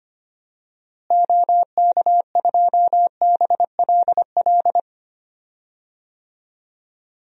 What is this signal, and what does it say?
Morse code, heard as a keyed tone, about 25 wpm, 700 Hz. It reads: OK2BLL
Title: DXCC-100 Call Signs - 6 Characters - Repeated 3x - 25wpm